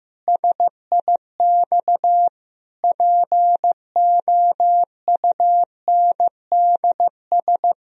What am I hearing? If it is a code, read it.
Morse code, heard as a keyed tone, about 15 wpm, 700 Hz. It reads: SIX POUNDS
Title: Sentences from Top 1000 Words - Review Part 2 - 15wpm